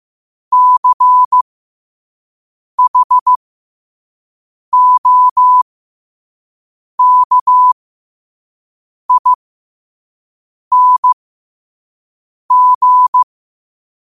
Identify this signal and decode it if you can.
Morse code, heard as a keyed tone, about 15 wpm, spaced as for 5 wpm, 1000 Hz. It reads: CHOKING
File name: qotdfiles / 5wpm